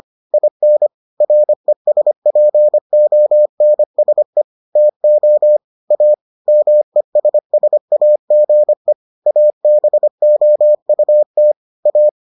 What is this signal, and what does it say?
Morse code, heard as a keyed tone, about 25 wpm, 600 Hz. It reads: IN RESPONSE TO A MESSAGE ABOUT A